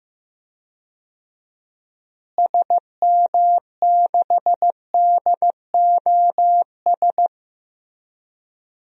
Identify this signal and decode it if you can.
Morse code, heard as a keyed tone, about 15 wpm, 700 Hz. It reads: SM6DOS
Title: DXCC-100 Call Signs - 6 Characters - Repeated 2x - 15wpm